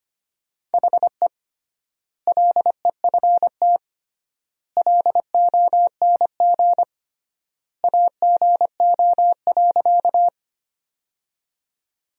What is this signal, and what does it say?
Morse code, heard as a keyed tone, about 25 wpm, 700 Hz. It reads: HE LEFT LONG AGO.
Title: 3x Word-Spacing - Sentences from Top 500 Words - 25wpm